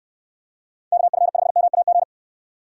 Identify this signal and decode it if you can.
Morse code, heard as a keyed tone, about 69 wpm, 700 Hz. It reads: KF5MAG